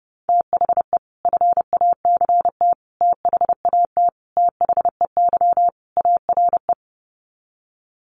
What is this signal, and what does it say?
Morse code, heard as a keyed tone, about 30 wpm, 700 Hz. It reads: THE FACT THAT THEY ARE
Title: Sets of 5 Words Encore2 30wpm